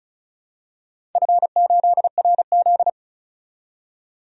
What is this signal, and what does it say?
Morse code, heard as a keyed tone, about 34 wpm, 700 Hz. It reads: F8RZ
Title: DXCC-100 Call Signs - 4 Characters - 35wpm